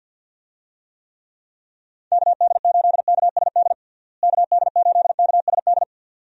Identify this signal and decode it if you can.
Morse code, heard as a keyed tone, about 48 wpm, 700 Hz. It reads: KD8KSD KD8KSD
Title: USA Call Signs - 6 Characters - Repeated 2x - 50wpm